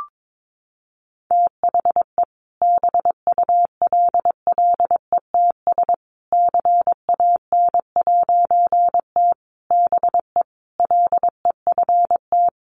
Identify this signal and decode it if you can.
Morse code, heard as a keyed tone, about 22 wpm, 700 Hz. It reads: THE BULLETS CAN'T BE LEFT
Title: A Sound Of Thunder 22wpm